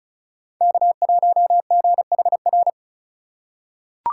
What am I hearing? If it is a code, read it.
Morse code, heard as a keyed tone, about 36 wpm, 700 Hz. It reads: K1GHR
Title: USA Call Signs - 5 Characters - 3x Character Spacing - 35wpm